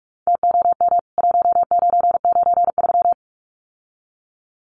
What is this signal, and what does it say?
Morse code, heard as a keyed tone, about 44 wpm, 700 Hz. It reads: TOM 1993
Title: CWT Exchange 45wpm